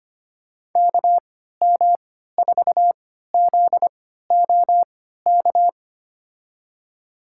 Morse code, heard as a keyed tone, about 25 wpm, spaced as for 14 wpm, 700 Hz. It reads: KM4ZOK
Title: USA Call Signs - 6 Characters - 3x Character Spacing - 25wpm